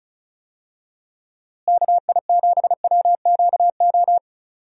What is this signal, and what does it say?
Morse code, heard as a keyed tone, about 36 wpm, 700 Hz. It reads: KI7WQO
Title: USA Call Signs - 6 Characters - Repeated 3x - 35wpm